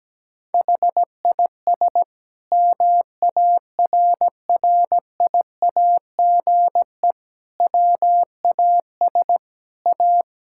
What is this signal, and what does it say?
Morse code, heard as a keyed tone, about 17 wpm, 700 Hz. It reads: HIS MARRIAGE WAS A
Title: Sentences from Top 1000 Words - Review Part 5 - 17wpm